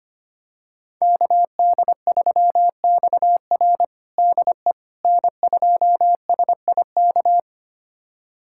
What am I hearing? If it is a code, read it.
Morse code, heard as a keyed tone, about 25 wpm, 700 Hz. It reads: KD3XR DE N2SIK
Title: USA Callsign DE Callsign - 5 Characters - 25wpm